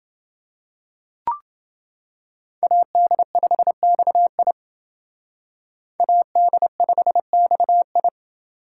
Morse code, heard as a keyed tone, about 29 wpm, 700 Hz. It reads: AD5XI AD5XI